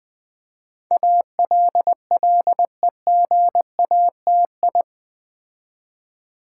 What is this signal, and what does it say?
Morse code, heard as a keyed tone, about 20 wpm, 700 Hz. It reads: ALLEGATI
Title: Build A Word - 4 Letters At A Time - 10-Letter Words - 20wpm